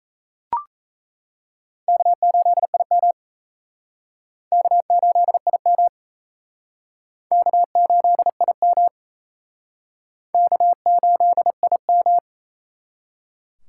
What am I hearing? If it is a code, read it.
Morse code, heard as a keyed tone, about 38 wpm, 700 Hz. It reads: K8IM K8IM K8IM K8IM